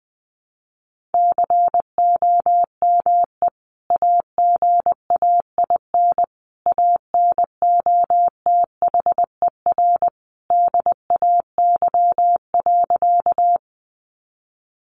Morse code, heard as a keyed tone, about 20 wpm, 700 Hz. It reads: COME AGAIN ANOTHER DAY.
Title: Sentences from Top 600 Words 20wpm